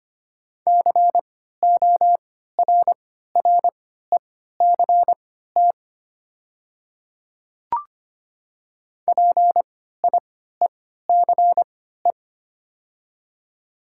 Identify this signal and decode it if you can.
Morse code, heard as a keyed tone, about 25 wpm, spaced as for 14 wpm, 700 Hz. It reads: CORRECT PIECE
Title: 3x Character Spacing - Top 500 Words - 25wpm